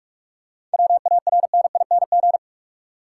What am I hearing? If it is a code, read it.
Morse code, heard as a keyed tone, about 46 wpm, 700 Hz. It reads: WARNING
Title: Seven Letter Words - 45wpm